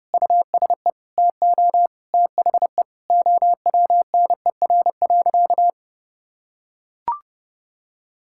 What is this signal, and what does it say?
Morse code, heard as a keyed tone, about 30 wpm, 700 Hz. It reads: USE TO THE OWNER.
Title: Sentences from Top 1600 Words 30wpm